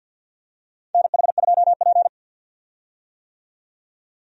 Morse code, heard as a keyed tone, about 48 wpm, 700 Hz. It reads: NH2P
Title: USA Call Signs - 4 Characters - 50wpm